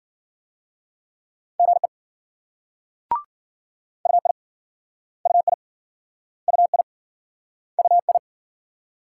Morse code, heard as a keyed tone, about 47 wpm, 700 Hz. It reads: GE UI UI UI UI
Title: Speed-Racing Multi-letter Phonemes - 40wpm